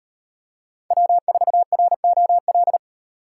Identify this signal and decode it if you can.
Morse code, heard as a keyed tone, about 34 wpm, 700 Hz. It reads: W4ROL